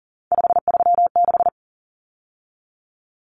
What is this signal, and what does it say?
Morse code, heard as a keyed tone, about 40 wpm, 700 Hz. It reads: 536
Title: Numbers - 3-Digit 40wpm